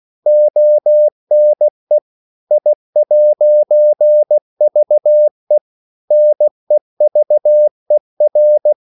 Morse code, heard as a keyed tone, about 16 wpm, 600 Hz. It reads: ONE I'VE NEVER